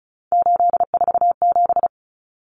Morse code, heard as a keyed tone, about 34 wpm, 700 Hz. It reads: 847
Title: Numbers - 3-Digit 35wpm